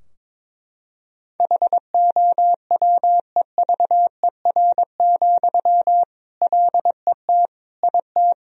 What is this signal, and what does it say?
Morse code, heard as a keyed tone, about 22 wpm, 700 Hz. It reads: HOWEVER, LET IT
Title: Sentences from Top 1000 Words - Review Part 2 - 22wpm